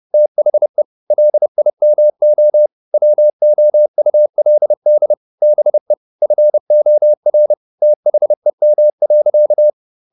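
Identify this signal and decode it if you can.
Morse code, heard as a keyed tone, about 30 wpm, 600 Hz. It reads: THE LIMO WOULD BE FOR THEM.
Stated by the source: Morse code at 30 wpm, the novel Never by Ken Follett in English, from CW med Gnister.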